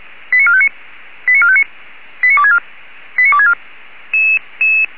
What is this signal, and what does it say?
Five-tone selective call (ZVEI), heard as quick runs of five tones, tones of about 70 ms. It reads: 89459 89459 89264 89264